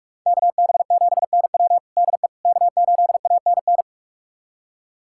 Morse code, heard as a keyed tone, about 42 wpm, 700 Hz. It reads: KB7NW DE K8ANN
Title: USA Callsign DE Callsign - 5 Characters - 45wpm